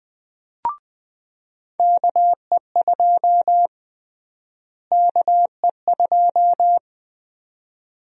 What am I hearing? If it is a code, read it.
Morse code, heard as a keyed tone, about 21 wpm, 700 Hz. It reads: KE2 KE2
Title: USA Call Sign Prefixes - Repeated 2x - 20wpm